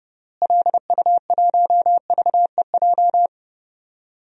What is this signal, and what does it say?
Morse code, heard as a keyed tone, about 30 wpm, 700 Hz. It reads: LU1VEJ
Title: DXCC-100 Call Signs - 6 Characters - Repeated 3x - 30wpm